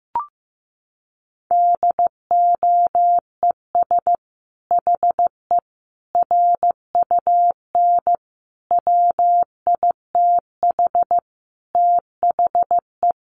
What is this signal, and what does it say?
Morse code, heard as a keyed tone, about 14 wpm, 700 Hz. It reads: DOES HE RUN WITH THE
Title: Sentences from Top 1300 Words 15wpm